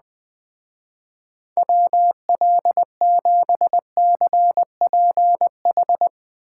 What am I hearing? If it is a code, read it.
Morse code, heard as a keyed tone, about 20 wpm, 700 Hz. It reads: WL7CPH
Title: Speed Racing - USA 2x3 Call Signs 17wpm